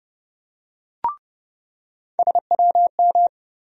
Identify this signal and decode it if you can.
Morse code, heard as a keyed tone, about 30 wpm, 700 Hz. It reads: SWM
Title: USA Call Sign Postfixes - 3 Letters 30wpm